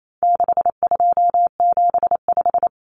Morse code, heard as a keyed tone, about 28 wpm, 700 Hz. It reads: 6275
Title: Common Cut Numbers - 4-Digit 28wpm